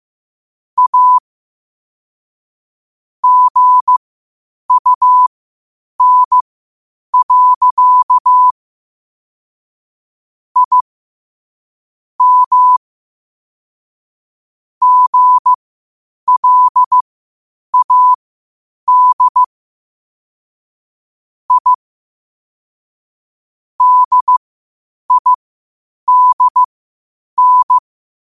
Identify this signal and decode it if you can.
Morse code, heard as a keyed tone, about 15 wpm, spaced as for 8 wpm, 1000 Hz. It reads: A GUN. I M GLAD I DIDN
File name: qotdfiles / 8wpm